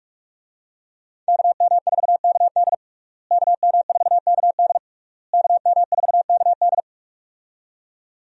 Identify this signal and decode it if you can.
Morse code, heard as a keyed tone, about 42 wpm, 700 Hz. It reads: KM4KD KM4KD KM4KD
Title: USA Call Signs - 5 Characters - Repeated 3x - 45wpm